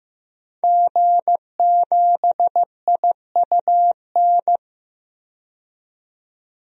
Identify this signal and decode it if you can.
Morse code, heard as a keyed tone, about 15 wpm, 700 Hz. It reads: G7IUN
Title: DXCC-100 Call Signs - 5 Characters - 15wpm